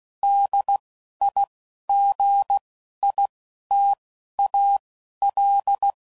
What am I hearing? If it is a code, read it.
Morse code, heard as a keyed tone, about 16 wpm, spaced as for 11 wpm, 800 Hz. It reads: DIGITAL
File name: CW Code Groups @10WPM / Technical Subjects / Gates Intro